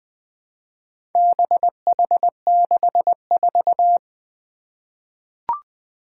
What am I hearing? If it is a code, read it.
Morse code, heard as a keyed tone, about 20 wpm, 700 Hz. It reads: BH64
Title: Four Challenging Letter-Number Combinations 20wpm